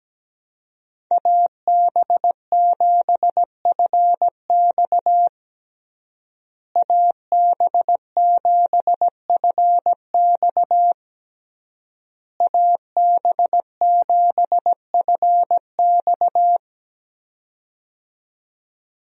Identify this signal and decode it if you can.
Morse code, heard as a keyed tone, about 17 wpm, 700 Hz. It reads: AB7FX AB7FX AB7FX
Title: USA Call Signs - 5 Characters - Repeated 3x - 17wpm